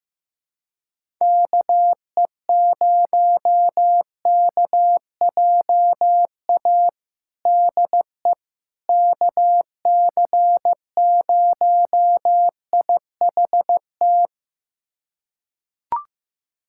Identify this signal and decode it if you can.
Morse code, heard as a keyed tone, about 15 wpm, 700 Hz. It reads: KE0KJA DE KC0IHT
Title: USA Callsign DE Callsign - 6 Characters - 15wpm